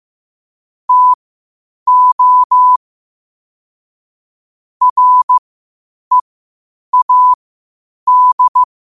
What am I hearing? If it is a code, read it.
Morse code, heard as a keyed tone, about 15 wpm, spaced as for 8 wpm, 1000 Hz. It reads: TO READ
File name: 8wpm